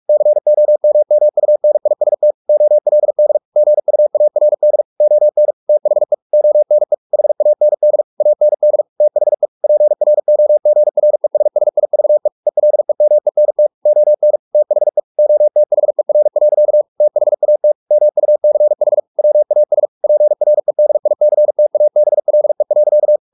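Morse code, heard as a keyed tone, about 44 wpm, 600 Hz. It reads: COMMUNIST OLD GUARD ON THE ONE HAND AND THE PROGRESSIVE ELEMENT ON THE OTHER. THAT MUCH WAS PREDICTABLE.
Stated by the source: Morse practice at 45 wpm, the novel Never by Ken Follett in English, from CW med Gnister.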